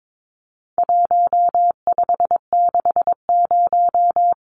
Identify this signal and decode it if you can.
Morse code, heard as a keyed tone, about 22 wpm, 700 Hz. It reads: 1560
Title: Numbers - Contest Style - 4-Digit 22wpm